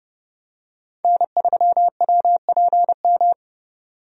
Morse code, heard as a keyed tone, about 32 wpm, 700 Hz. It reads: N3WPM
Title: USA Call Signs - 5 Characters - 30wpm